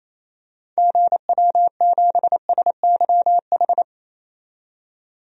Morse code, heard as a keyed tone, about 28 wpm, 700 Hz. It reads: GW7SYH
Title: DXCC-100 Call Signs - 6 Characters - 28wpm